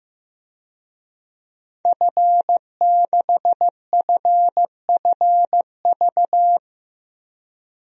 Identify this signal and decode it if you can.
Morse code, heard as a keyed tone, about 15 wpm, 700 Hz. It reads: F6FFV
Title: DXCC-100 Call Signs - 5 Characters - Repeated 2x - 15wpm